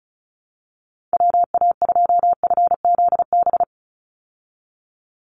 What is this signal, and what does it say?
Morse code, heard as a keyed tone, about 34 wpm, 700 Hz. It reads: WA2FZB
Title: Call Signs Repeated 2x - 35wpm